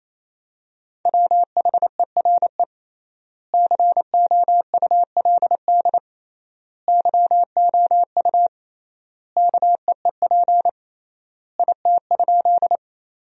Morse code, heard as a keyed tone, about 28 wpm, 700 Hz. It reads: WHERE COULD YOU KEEP IT?
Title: 3x Word-Spacing - Sentences from Top 300 Words - 28wpm